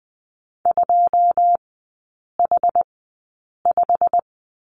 Morse code, heard as a keyed tone, about 20 wpm, spaced as for 8 wpm, 700 Hz. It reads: 2H5
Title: 127 Lesson 21e - Number 6 - Three Characters 20wpm